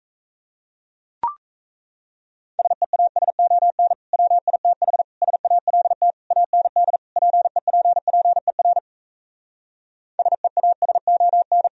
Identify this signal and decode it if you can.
Morse code, heard as a keyed tone, about 38 wpm, 700 Hz. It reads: SEASON WITH SALT AND PEPPER SEASON